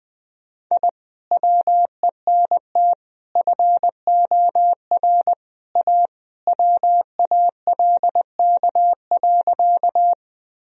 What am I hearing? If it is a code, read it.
Morse code, heard as a keyed tone, about 19 wpm, 700 Hz. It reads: I WENT FOR A WALK.
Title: Sentences from Top 1000 Words - Review Part 1 - 20wpm